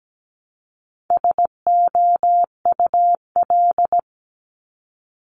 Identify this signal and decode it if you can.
Morse code, heard as a keyed tone, about 17 wpm, 700 Hz. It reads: SOUL
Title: Multi-letter Phonemes Repeated 3x 17wpm